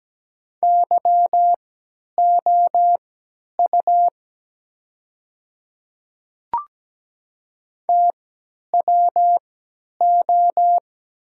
Morse code, heard as a keyed tone, about 17 wpm, spaced as for 10 wpm, 700 Hz. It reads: YOU TWO
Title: 3x Character Spacing - Top 100 Words - 17wpm